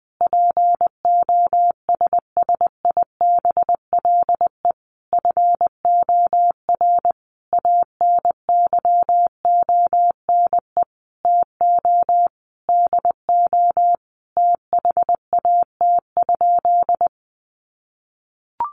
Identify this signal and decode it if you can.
Morse code, heard as a keyed tone, about 20 wpm, 700 Hz. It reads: POSSIBLE FOR ANYONE TO DO THAT?
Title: Sentences from Top 900 Words 20wpm